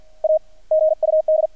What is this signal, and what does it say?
Morse code, heard as a keyed tone, about 50 wpm, 630 Hz. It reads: A GUD
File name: archived / CRSnet